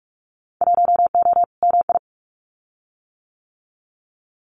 Morse code, heard as a keyed tone, about 44 wpm, 700 Hz. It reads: 1O MI
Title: Winter Field Day Exchanges Encore 45wpm